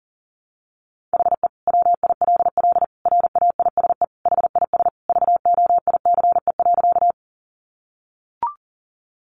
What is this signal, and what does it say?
Morse code, heard as a keyed tone, about 38 wpm, 700 Hz. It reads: HE WILL RAISE HIS VOICE.
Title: Sentences from Top 1400 Words 40wpm